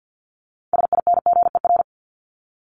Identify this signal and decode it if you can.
Morse code, heard as a keyed tone, about 50 wpm, 700 Hz. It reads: SINGER
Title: Top 1501-1750 Words 50wpm